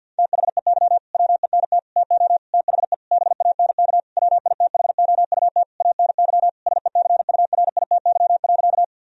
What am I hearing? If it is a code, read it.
Morse code, heard as a keyed tone, about 50 wpm, 700 Hz. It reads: THEY WENT TO THE BANK WITHOUT ANY SECURITY.
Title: Sentences from Top 1000 Words - Review Part 4 - 50wpm